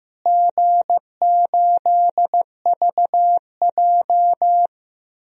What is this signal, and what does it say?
Morse code, heard as a keyed tone, about 15 wpm, 700 Hz. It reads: G8VJ
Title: DXCC-100 Call Signs - 4 Characters - Repeated 3x - 15wpm